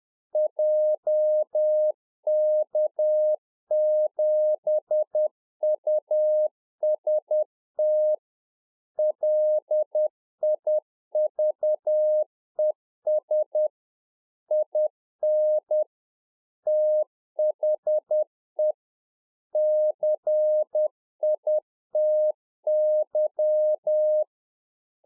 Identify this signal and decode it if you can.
Morse code, heard as a keyed tone, about 10 wpm, 610 Hz. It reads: JK7UST LIVES IN THE CITY